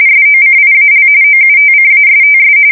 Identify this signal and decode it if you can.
RTTY, about 45.45 baud. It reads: SOCIAL OUTCASTS
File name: fsk170 / usnavy2